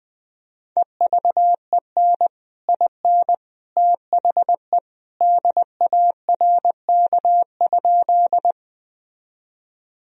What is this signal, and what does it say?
Morse code, heard as a keyed tone, about 19 wpm, 700 Hz. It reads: EVEN IN THE DARK?
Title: Sentences from Top 1000 Words - Review Part 1 - 20wpm